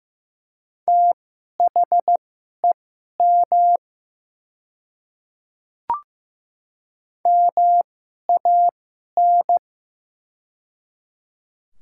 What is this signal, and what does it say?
Morse code, heard as a keyed tone, about 15 wpm, spaced as for 11 wpm, 700 Hz. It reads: THEM MAN
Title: 2x Character Spacing - Top 200 Words - 15wpm